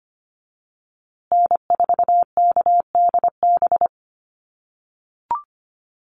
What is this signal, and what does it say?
Morse code, heard as a keyed tone, about 25 wpm, 700 Hz. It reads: N4KDB
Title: USA Call Signs Repeated 2x 25wpm